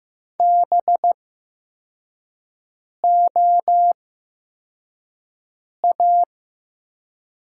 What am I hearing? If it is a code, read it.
Morse code, heard as a keyed tone, about 15 wpm, spaced as for 4 wpm, 700 Hz. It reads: BOA
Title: 8x Character Spacing - Top 500 Words - 15wpm